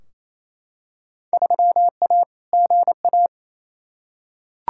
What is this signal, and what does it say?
Morse code, heard as a keyed tone, about 28 wpm, 700 Hz. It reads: 3A GA
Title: Field Day Exchanges Encore - 28wpm